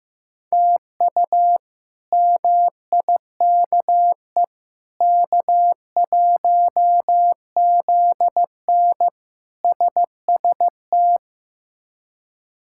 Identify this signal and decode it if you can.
Morse code, heard as a keyed tone, about 15 wpm, 700 Hz. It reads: TU MIKE K1ZN SST
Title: K1USN SST QSO Practice - 15wpm